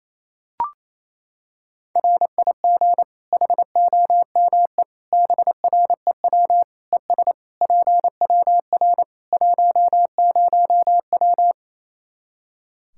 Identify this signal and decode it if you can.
Morse code, heard as a keyed tone, about 28 wpm, 700 Hz. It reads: RIG HOME BREW ES PWR 10W
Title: Practice QSO - Exchange 2 - Info - 28wpm